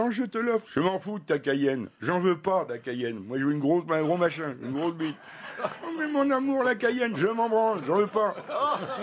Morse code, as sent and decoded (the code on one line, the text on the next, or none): none